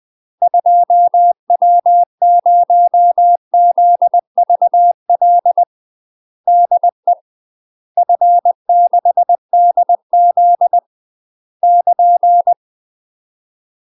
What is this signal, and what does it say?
Morse code, heard as a keyed tone, about 20 wpm, 700 Hz. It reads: 2W0ZVL DE F6DZ <KN>
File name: UK MASTER QSOs TRAINING / 20WPM / UK MASTER QSOs 0040